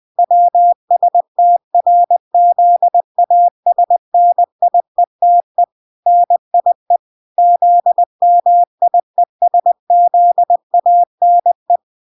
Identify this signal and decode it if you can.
Morse code, heard as a keyed tone, about 20 wpm, 700 Hz. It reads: WSTRZASNIETE NIE ZMIESZANE
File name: 20wpm